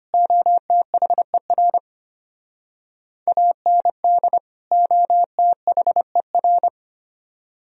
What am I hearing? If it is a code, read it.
Morse code, heard as a keyed tone, about 33 wpm, 700 Hz. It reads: OTHER AND OTHER